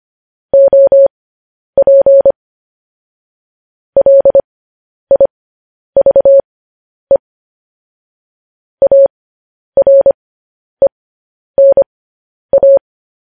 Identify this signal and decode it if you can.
Morse code, heard as a keyed tone, about 25 wpm, spaced as for 10 wpm, 550 Hz. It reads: OP LIVE ARENA
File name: podcast / News headlines-1x-25x10